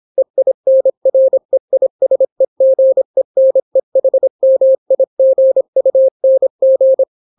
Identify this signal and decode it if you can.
Morse code, heard as a keyed tone, about 26 wpm, 530 Hz. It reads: EINREISEGENEHMIGUNG